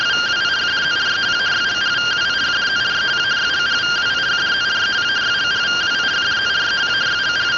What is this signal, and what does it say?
Two-tone teleprinter signal, about 170 Hz shift.